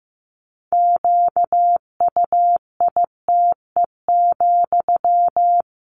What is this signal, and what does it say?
Morse code, heard as a keyed tone, about 15 wpm, 700 Hz. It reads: QUITE,
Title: Sentences from Top 1500 Words 15wpm